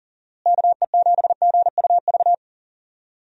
Morse code, heard as a keyed tone, about 40 wpm, 700 Hz. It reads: KE7GUV
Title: USA Call Signs - 6 Characters - Repeated 2x - 40wpm